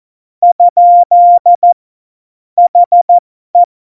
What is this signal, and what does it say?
Morse code, heard as a keyed tone, about 14 wpm, spaced as for 12 wpm, 700 Hz. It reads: ? HE
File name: Morsekurs / Witze